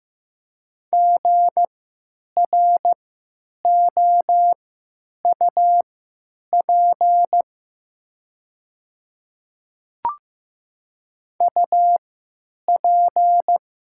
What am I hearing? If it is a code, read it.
Morse code, heard as a keyed tone, about 15 wpm, spaced as for 9 wpm, 700 Hz. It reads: GROUP UP
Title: 3x Character Spacing - Top 500 Words - 15wpm